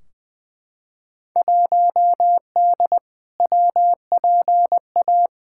Morse code, heard as a keyed tone, about 20 wpm, 700 Hz. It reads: 1D WPA
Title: Field Day Exchanges - 20wpm